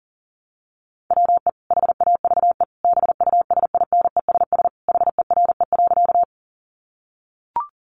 Morse code, heard as a keyed tone, about 40 wpm, 700 Hz. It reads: WE HAVE BUSINESS HERE.
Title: Sentences from Top 600 Words 40wpm